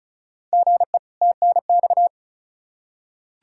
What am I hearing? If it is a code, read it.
Morse code, heard as a keyed tone, about 36 wpm, 700 Hz. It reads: GE TNX
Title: Practice QSO - Exchange 1 - Beginning - 35wpm